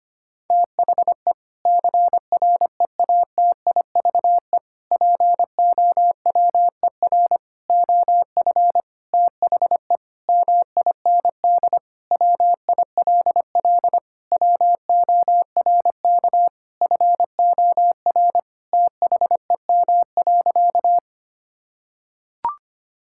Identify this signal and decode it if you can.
Morse code, heard as a keyed tone, about 25 wpm, 700 Hz. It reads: THE CREATIVE POWER OF THE MIND WILL WORK FOR THEM.
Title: Sentences from Top 1000 Words - Review Part 5 - 25wpm